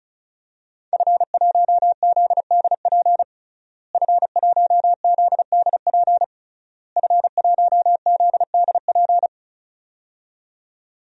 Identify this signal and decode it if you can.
Morse code, heard as a keyed tone, about 36 wpm, 700 Hz. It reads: F1ZDP F1ZDP F1ZDP
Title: DXCC-100 Call Signs - 5 Characters - Repeated 3x - 35wpm